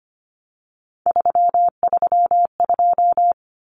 Morse code, heard as a keyed tone, about 25 wpm, 700 Hz. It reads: 332
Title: Numbers - 3-Digit 25wpm